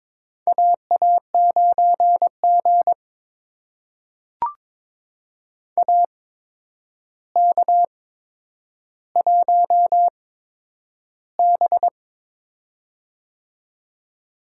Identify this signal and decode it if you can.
Morse code, heard as a keyed tone, about 22 wpm, spaced as for 9 wpm, 700 Hz. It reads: AA9G AK1B
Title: USA Call Signs - 4 Characters - 8x Character Spacing - 22wpm